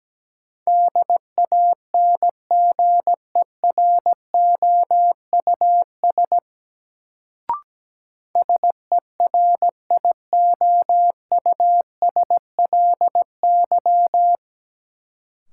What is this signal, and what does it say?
Morse code, heard as a keyed tone, about 17 wpm, 700 Hz. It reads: DANGEROUS SERIOUSLY
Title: Sentences from Top 1200 Words - New Word Drill - 17wpm